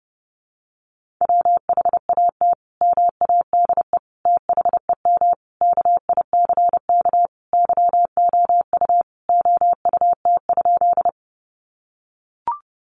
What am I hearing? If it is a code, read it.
Morse code, heard as a keyed tone, about 28 wpm, 700 Hz. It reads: WHAT MADE THEM KICK YOU OUT?
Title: Sentences from Top 1300 Words 30wpm